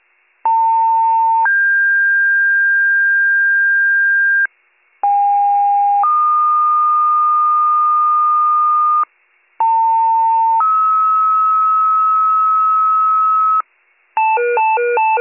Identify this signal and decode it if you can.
Two-tone page, heard as a one-second tone, then a three-second tone, about 900 then 1600 Hz, about 810 then 1220 Hz, about 900 then 1290 Hz. It is heard from 0.5 to 13.6 s.